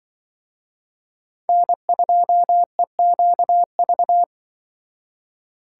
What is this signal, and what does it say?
Morse code, heard as a keyed tone, about 24 wpm, 700 Hz. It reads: N2EQV
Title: Speed Racing - USA 1x3 Call Signs 20wpm